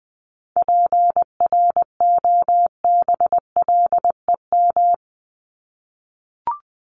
Morse code, heard as a keyed tone, about 20 wpm, 700 Hz. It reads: PROBLEM
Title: Top 500 Words Repeated 20wpm